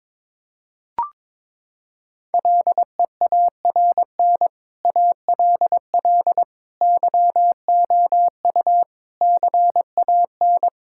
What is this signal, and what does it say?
Morse code, heard as a keyed tone, about 22 wpm, 700 Hz. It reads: LEARN ALL YOU CAN
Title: Sentences from Top 1000 Words - Review Part 5 - 22wpm